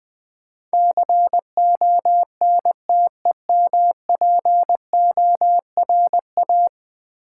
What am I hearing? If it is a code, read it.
Morse code, heard as a keyed tone, about 20 wpm, 700 Hz. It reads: CONTEMPORA
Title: Build A Word - 5 Letters At A Time - 12-Letter Words - 20wpm